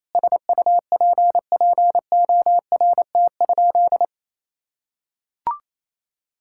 Morse code, heard as a keyed tone, about 28 wpm, 700 Hz. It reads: SUPPORT?
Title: Sentences from Top 1000 Words - Review Part 5 - 28wpm